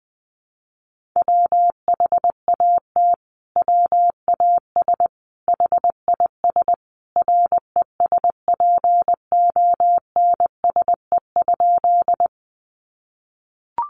Morse code, heard as a keyed tone, about 20 wpm, 700 Hz. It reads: WHAT WAS HIS RESPONSE?
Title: Sentences from Top 1400 Words 20wpm